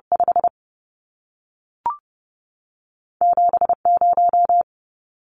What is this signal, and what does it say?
Morse code, heard as a keyed tone, about 30 wpm, 700 Hz. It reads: 5 7 0
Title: Numbers - 2-Digit 30wpm